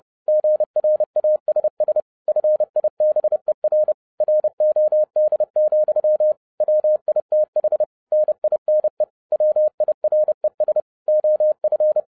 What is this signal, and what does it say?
Morse code, heard as a keyed tone, about 30 wpm, 610 Hz. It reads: GRASS FIBER ROD, WITH NINE WIRES OF